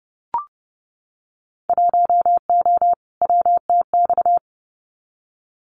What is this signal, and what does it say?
Morse code, heard as a keyed tone, about 30 wpm, 700 Hz. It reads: 1O WTX
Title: Winter Field Day Exchanges Encore 30wpm